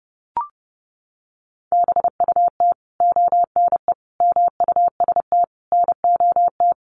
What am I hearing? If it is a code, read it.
Morse code, heard as a keyed tone, about 30 wpm, 700 Hz. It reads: BUT ONE MUST NOT
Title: Sentences from Top 1100 Words 30wpm